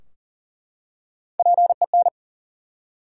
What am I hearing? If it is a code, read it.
Morse code, heard as a keyed tone, about 40 wpm, 700 Hz. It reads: PEN
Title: Three Letter Words - 40wpm